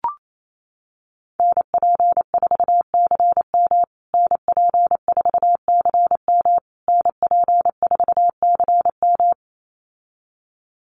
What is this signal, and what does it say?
Morse code, heard as a keyed tone, about 28 wpm, 700 Hz. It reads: NP4CM NP4CM NP4CM
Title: USA Call Signs Repeated 3x 28wpm